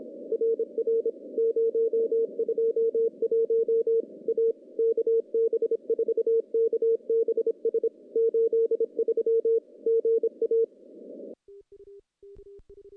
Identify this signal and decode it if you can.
Morse code, heard as a keyed tone, about 26 wpm, 460 Hz. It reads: RR 021 A KB4KBS 83 GA
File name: contests / CW-SS-k5tr- -0200